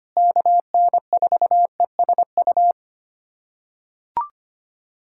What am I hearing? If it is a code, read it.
Morse code, heard as a keyed tone, about 25 wpm, 700 Hz. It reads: KN4ESU
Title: USA Callsign DE Callsign 25wpm